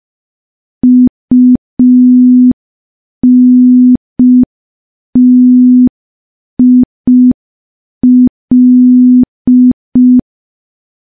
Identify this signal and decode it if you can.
Morse code, heard as a keyed tone, about 5 wpm, 250 Hz. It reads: UNTIL
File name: morse / morse-training-05WPM